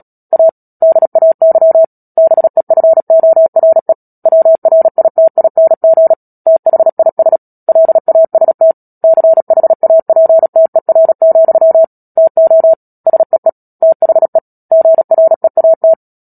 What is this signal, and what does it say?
Morse code, heard as a keyed tone, about 37 wpm, 650 Hz. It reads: A DAY BEFORE WRITING THIS LAST CHAPTER, TO SEE THE GREAT
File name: Book Two / 10-Chapter 10 - Epilogue - 36 WPM